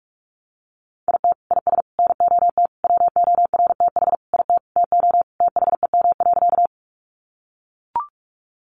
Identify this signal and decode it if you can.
Morse code, heard as a keyed tone, about 44 wpm, 700 Hz. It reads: IT IS NOT WORTH IT TO THEM.
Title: Sentences from Top 800 Words 45wpm